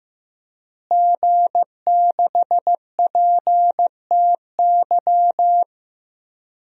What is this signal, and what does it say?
Morse code, heard as a keyed tone, about 15 wpm, 700 Hz. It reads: G6PTY
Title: DXCC-100 Call Signs - 5 Characters - Repeated 2x - 15wpm